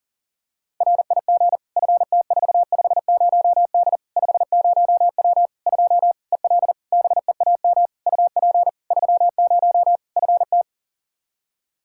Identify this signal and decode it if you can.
Morse code, heard as a keyed tone, about 40 wpm, 700 Hz. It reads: RIG FT450D 50W 2 EL BEAM UP 30 FT
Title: Practice QSO - Exchange 2 - Info - 40wpm